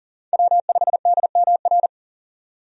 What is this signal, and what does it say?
Morse code, heard as a keyed tone, about 40 wpm, 700 Hz. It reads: W5DMR
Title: USA Call Signs - 5 Characters - 40wpm